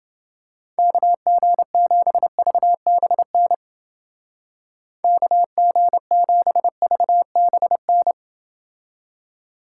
Morse code, heard as a keyed tone, about 30 wpm, 700 Hz. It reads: KG7VBN KG7VBN